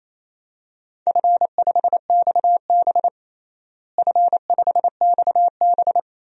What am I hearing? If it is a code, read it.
Morse code, heard as a keyed tone, about 28 wpm, 700 Hz. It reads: F5XB F5XB
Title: DXCC-100 Call Signs - 4 Characters - Repeated 3x - 28wpm